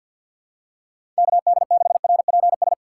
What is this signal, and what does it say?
Morse code, heard as a keyed tone, about 52 wpm, 700 Hz. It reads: KD6RPS
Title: USA Call Signs - 6 Characters - 4x Character Spacing - 50wpm